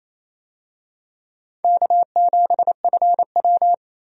Morse code, heard as a keyed tone, about 28 wpm, 700 Hz. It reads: K7FW
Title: USA Call Signs - 4 Characters - 28wpm